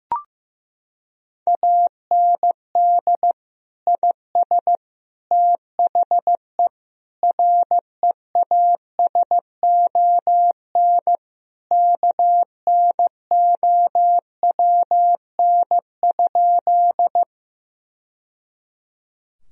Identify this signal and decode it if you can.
Morse code, heard as a keyed tone, about 14 wpm, 700 Hz. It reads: AND IS THE REASON KNOWN?
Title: Sentences from Top 1000 Words - Review Part 3 - 15wpm